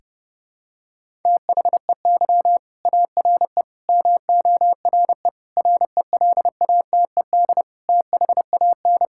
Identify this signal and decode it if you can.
Morse code, heard as a keyed tone, about 30 wpm, 700 Hz. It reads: THEY ARE MORE RELATED THAN
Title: Sentences from Top 1000 Words - Review Part 5 - 30wpm